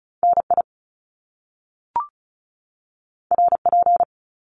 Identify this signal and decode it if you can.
Morse code, heard as a keyed tone, about 36 wpm, 700 Hz. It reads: NI RP
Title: Top Two Letter Combinations in Words 35wpm